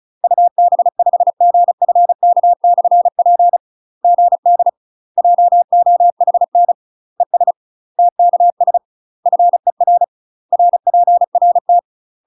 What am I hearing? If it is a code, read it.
Morse code, heard as a keyed tone, about 36 wpm, 700 Hz. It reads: UB5GFK/P GD JOHN ES TKS FER RPRT
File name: UK MASTER QSOs TRAINING / 35WPM / UK MASTER QSOs 0026